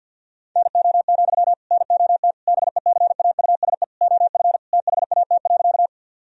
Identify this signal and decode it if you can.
Morse code, heard as a keyed tone, about 50 wpm, 700 Hz. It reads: NO, NOT BECAUSE OF THAT.
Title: Sentences from Top 1000 Words - Review Part 3 - 50wpm